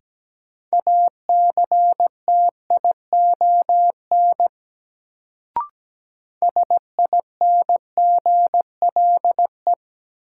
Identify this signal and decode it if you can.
Morse code, heard as a keyed tone, about 17 wpm, 700 Hz. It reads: ACTION SINGLE
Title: Sentences from Top 700 Words - New Word Drill - 17wpm